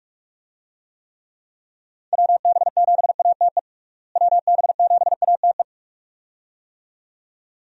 Morse code, heard as a keyed tone, about 47 wpm, 700 Hz. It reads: WB7ATE WB7ATE
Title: USA Call Signs - 6 Characters - Repeated 2x - 45wpm